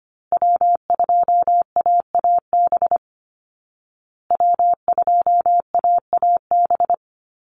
Morse code, heard as a keyed tone, about 25 wpm, 700 Hz. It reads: W2AAB W2AAB
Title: Call Signs Repeated 2x - 25wpm